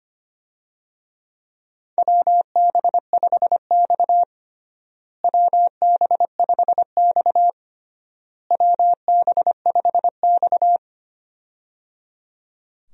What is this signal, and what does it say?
Morse code, heard as a keyed tone, about 25 wpm, 700 Hz. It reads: WB5X WB5X WB5X
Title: USA Call Signs - 4 Characters - Repeated 3x - 25wpm